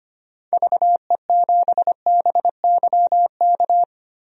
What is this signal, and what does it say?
Morse code, heard as a keyed tone, about 25 wpm, 700 Hz. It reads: VE7BYK
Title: DXCC-100 Call Signs - 6 Characters - Repeated 2x - 25wpm